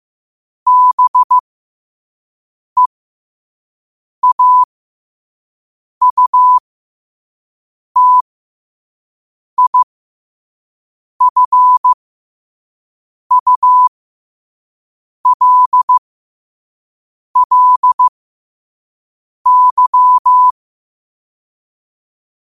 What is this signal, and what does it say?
Morse code, heard as a keyed tone, about 15 wpm, spaced as for 5 wpm, 1000 Hz. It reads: BEAUTIFULLY
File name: qotdfiles / 5wpm